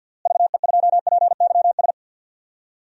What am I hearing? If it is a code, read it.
Morse code, heard as a keyed tone, about 52 wpm, 700 Hz. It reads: VE2PYS
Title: DXCC-100 Call Signs - 6 Characters - Repeated 3x - 50wpm